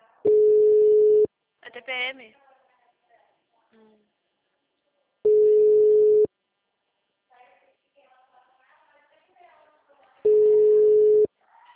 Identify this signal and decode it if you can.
Ringback tone (Europe, 425 Hz), with speech.
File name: out- -2007- - - .10